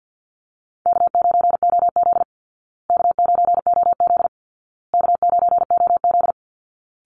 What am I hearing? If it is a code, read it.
Morse code, heard as a keyed tone, about 48 wpm, 700 Hz. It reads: K9OZ K9OZ K9OZ
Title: Easy Call Signs Repeated 3x - 50wpm